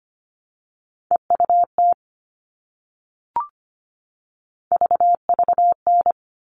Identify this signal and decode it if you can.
Morse code, heard as a keyed tone, about 25 wpm, 700 Hz. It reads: EUT VVN
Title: All Cut Numbers - 3-Digit 25wpm